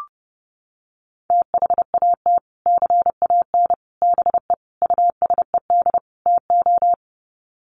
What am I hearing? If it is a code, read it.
Morse code, heard as a keyed tone, about 30 wpm, 700 Hz. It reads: THAT CAN BE USED TO
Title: Sets of 5 Words Encore 30wpm